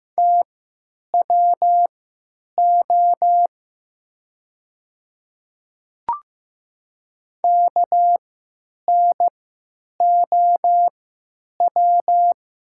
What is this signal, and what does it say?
Morse code, heard as a keyed tone, about 15 wpm, spaced as for 9 wpm, 700 Hz. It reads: TWO KNOW
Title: 3x Character Spacing - Top 300 Words - 15wpm